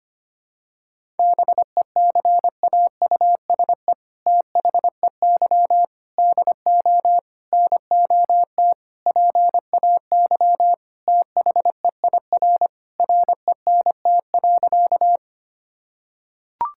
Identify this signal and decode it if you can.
Morse code, heard as a keyed tone, about 25 wpm, 700 Hz. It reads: BECAUSE THEY DO NOT PAY THEIR RENT.
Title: Sentences from Top 1000 Words - Review Part 5 - 25wpm